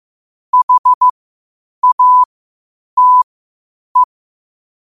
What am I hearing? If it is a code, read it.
Morse code, heard as a keyed tone, about 15 wpm, spaced as for 8 wpm, 1000 Hz. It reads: HATE
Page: New quotes every day in morse code at 8 Words per minute.